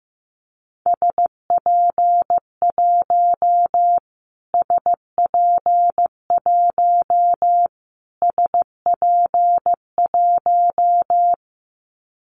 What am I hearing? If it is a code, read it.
Morse code, heard as a keyed tone, about 15 wpm, 700 Hz. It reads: SP1 SP1 SP1
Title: Callsign Prefixes for Top 25 DXCC - Repeated 3x - 15wpm